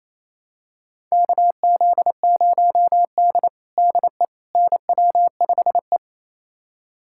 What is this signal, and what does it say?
Morse code, heard as a keyed tone, about 27 wpm, 700 Hz. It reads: KZ0D DE NW5E
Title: USA Callsign DE Callsign - 4 Characters - 28wpm